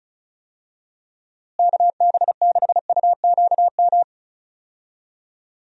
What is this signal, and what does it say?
Morse code, heard as a keyed tone, about 36 wpm, 700 Hz. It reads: KB6UQM
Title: USA Call Signs - 6 Characters - Repeated 2x - 35wpm